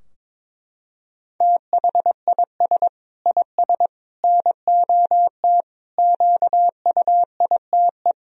Morse code, heard as a keyed tone, about 22 wpm, 700 Hz. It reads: THIS IS NOT QUITE
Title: Sentences from Top 1000 Words - Review Part 2 - 22wpm